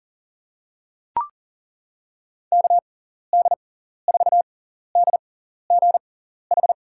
Morse code, heard as a keyed tone, about 40 wpm, spaced as for 14 wpm, 700 Hz. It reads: KD4DGH
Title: USA Call Signs - 6 Characters - 6x Character Spacing - 40wpm